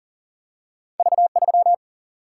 Morse code, heard as a keyed tone, about 39 wpm, 700 Hz. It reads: V3
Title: Single-Letter Single-Number 40wpm